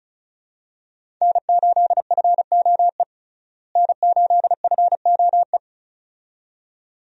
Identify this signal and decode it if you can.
Morse code, heard as a keyed tone, about 36 wpm, 700 Hz. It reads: N8FOE N8FOE
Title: USA Call Signs - 5 Characters - Repeated 2x - 35wpm